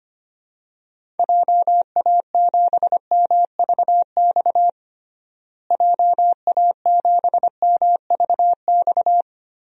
Morse code, heard as a keyed tone, about 25 wpm, 700 Hz. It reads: JA7MVX JA7MVX
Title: DXCC-100 Call Signs - 6 Characters - Repeated 3x - 25wpm